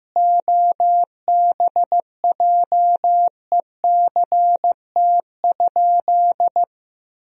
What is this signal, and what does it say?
Morse code, heard as a keyed tone, about 15 wpm, 700 Hz. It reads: OBJECT?
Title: 2x Word-Spacing - Sentences from Top 500 Words - 15wpm